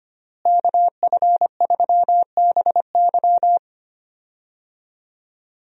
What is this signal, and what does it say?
Morse code, heard as a keyed tone, about 25 wpm, 700 Hz. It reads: KF3BY
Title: USA Call Signs - 5 Characters - Repeated 3x - 25wpm